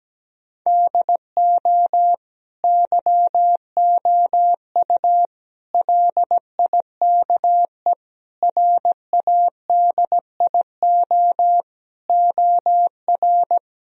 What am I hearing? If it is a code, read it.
Morse code, heard as a keyed tone, about 17 wpm, 700 Hz. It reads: DO YOU LIKE RADIO OR
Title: Sentences from Top 1000 Words - Review Part 4 - 17wpm